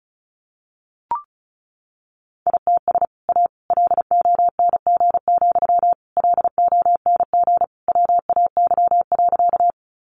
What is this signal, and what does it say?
Morse code, heard as a keyed tone, about 34 wpm, 700 Hz. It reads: ITS A LONG, LONG WAY.
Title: Sentences from Top 600 Words 35wpm